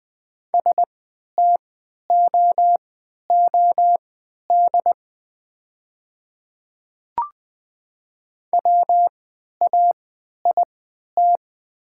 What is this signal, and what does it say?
Morse code, heard as a keyed tone, about 20 wpm, spaced as for 11 wpm, 700 Hz. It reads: STOOD WAIT
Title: 3x Character Spacing - Top 500 Words - 20wpm